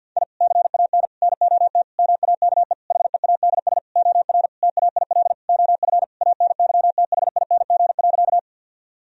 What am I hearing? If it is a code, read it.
Morse code, heard as a keyed tone, about 50 wpm, 700 Hz. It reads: I CAN NOT MAKE HEADS OR TAIL OF ANYTHING.
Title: Sentences from Top 1000 Words - Review Part 3 - 50wpm